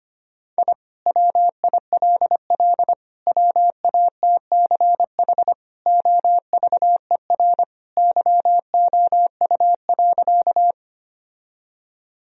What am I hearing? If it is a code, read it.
Morse code, heard as a keyed tone, about 25 wpm, 700 Hz. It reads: I WILL WATCH OVER YOU.
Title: Sentences from Top 1000 Words - Review Part 1 - 25wpm